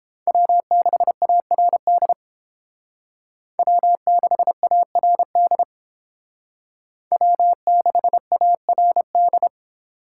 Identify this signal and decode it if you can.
Morse code, heard as a keyed tone, about 31 wpm, 700 Hz. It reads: W6ARD W6ARD W6ARD